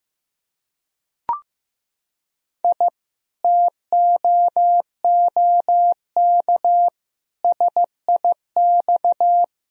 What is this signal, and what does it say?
Morse code, heard as a keyed tone, about 16 wpm, 700 Hz. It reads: I TOOK SIX
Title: Sentences from Top 1000 Words - Review Part 2 - 15wpm